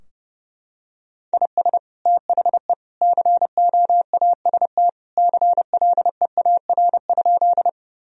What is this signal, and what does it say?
Morse code, heard as a keyed tone, about 32 wpm, 700 Hz. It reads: IS THE COAST CLEAR?
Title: Sentences from Top 1000 Words - Review Part 4 - 30wpm